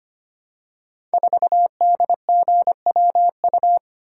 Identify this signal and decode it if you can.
Morse code, heard as a keyed tone, about 25 wpm, 700 Hz. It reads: 4DGWU
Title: Code Groups - 5 Letter:Numbers - 25wpm